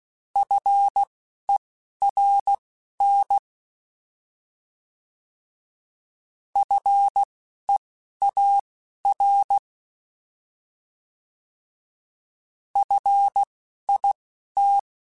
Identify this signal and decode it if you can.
Morse code, heard as a keyed tone, about 16 wpm, spaced as for 11 wpm, 800 Hz. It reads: FERN FEAR FIT
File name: Alphabetical / Alphabetical mp3s / Fs @10